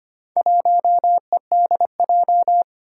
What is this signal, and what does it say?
Morse code, heard as a keyed tone, about 25 wpm, 700 Hz. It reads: 1EDJ
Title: Four Challenging Letter-Number Combinations 25wpm